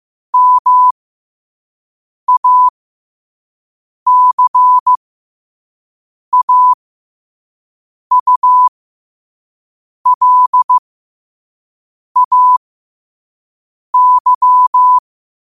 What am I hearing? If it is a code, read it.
Morse code, heard as a keyed tone, about 15 wpm, spaced as for 5 wpm, 1000 Hz. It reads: MACAULAY
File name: qotdfiles / 5wpm